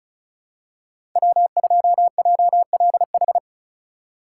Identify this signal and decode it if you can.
Morse code, heard as a keyed tone, about 36 wpm, 700 Hz. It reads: W2JLH
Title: USA Call Signs - 5 Characters - 4x Character Spacing - 35wpm